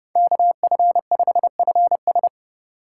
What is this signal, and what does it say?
Morse code, heard as a keyed tone, about 30 wpm, 700 Hz. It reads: KF5FS
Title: USA Call Signs - 5 Characters - 4x Character Spacing - 30wpm